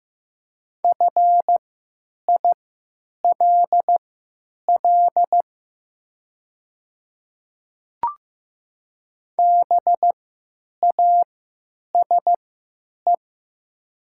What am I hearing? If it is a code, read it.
Morse code, heard as a keyed tone, about 15 wpm, spaced as for 9 wpm, 700 Hz. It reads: FILL BASE
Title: 3x Character Spacing - Top 500 Words - 15wpm